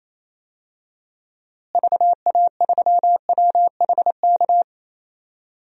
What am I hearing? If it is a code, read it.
Morse code, heard as a keyed tone, about 28 wpm, 700 Hz. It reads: VA3WHK
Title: DXCC-100 Call Signs - 6 Characters - Repeated 2x - 28wpm